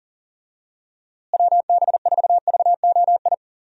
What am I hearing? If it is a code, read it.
Morse code, heard as a keyed tone, about 40 wpm, 700 Hz. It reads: WB4VOI
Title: USA Call Signs - 6 Characters - Repeated 3x - 40wpm